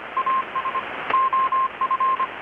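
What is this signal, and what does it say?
Morse code, heard as a keyed tone, about 25 wpm, 1060 Hz, with radio noise. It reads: AS OF
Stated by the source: Morse from KPH's Night of Nights XX maritime radiotelegraphy event, received in Chiapas, Mexico about 3,800 km away on a Kenwood R-600 with a grounded 10 m long-wire antenna.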